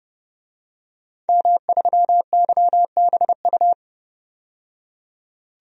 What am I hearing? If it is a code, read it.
Morse code, heard as a keyed tone, about 31 wpm, 700 Hz. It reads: M3YBU
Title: DXCC-100 Call Signs - 5 Characters - 30wpm